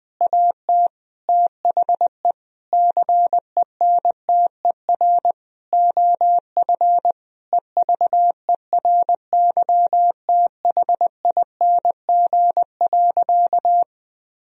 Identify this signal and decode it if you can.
Morse code, heard as a keyed tone, about 20 wpm, 700 Hz. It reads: AT THE CENTER OF EVERYTHING.
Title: Sentences from Top 1000 Words - Review Part 2 - 20wpm